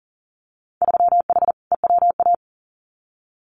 Morse code, heard as a keyed tone, about 38 wpm, 700 Hz. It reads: 3H EWA
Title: Winter Field Day Exchanges Encore 40wpm